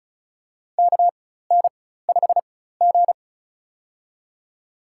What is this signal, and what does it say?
Morse code, heard as a keyed tone, about 36 wpm, spaced as for 16 wpm, 700 Hz. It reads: KN5G
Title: USA Call Signs - 4 Characters - 4x Character Spacing - 35wpm